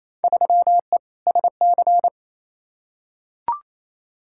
Morse code, heard as a keyed tone, about 28 wpm, 700 Hz. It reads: 3E SC
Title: Field Day Exchanges Encore - 28wpm